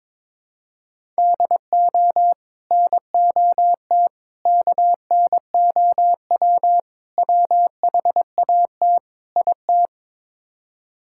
Morse code, heard as a keyed tone, about 22 wpm, 700 Hz. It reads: DO NOT KNOW WHAT IT
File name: practice / 22wpm / Build A Phrase - 5-Word Phrases - 22wpm